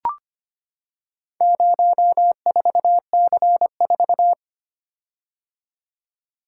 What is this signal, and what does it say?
Morse code, heard as a keyed tone, about 25 wpm, 700 Hz. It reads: 04C4
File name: practice / 25wpm / Four Letter-Number Combinations - 25wpm